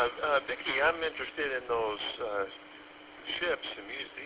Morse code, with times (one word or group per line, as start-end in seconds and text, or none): none